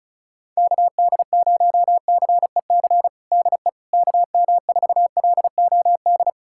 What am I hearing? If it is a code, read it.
Morse code, heard as a keyed tone, about 35 wpm, 700 Hz. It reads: KD0CEC DE KM4LOD
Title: USA Callsign DE Callsign - 6 Characters - 35wpm